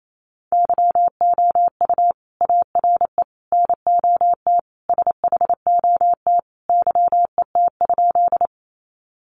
Morse code, heard as a keyed tone, about 28 wpm, 700 Hz. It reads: YOU ARE NOT SHOT YET?
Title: Sentences from Top 1100 Words 28wpm